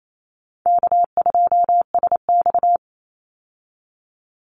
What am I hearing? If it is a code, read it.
Morse code, heard as a keyed tone, about 28 wpm, 690 Hz. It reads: K2SX
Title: Call Signs Repeated 3x - 28wpm